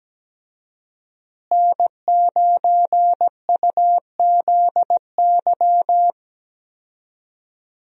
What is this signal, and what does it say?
Morse code, heard as a keyed tone, about 17 wpm, 700 Hz. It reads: N9UZY
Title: USA Call Signs - 5 Characters - Repeated 3x - 17wpm